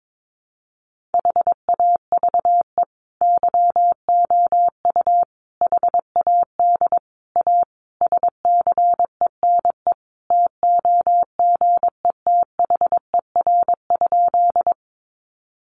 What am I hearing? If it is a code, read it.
Morse code, heard as a keyed tone, about 22 wpm, 700 Hz. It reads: HAVE YOU HAD A SCENE TOGETHER?
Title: Sentences from Top 1100 Words 22wpm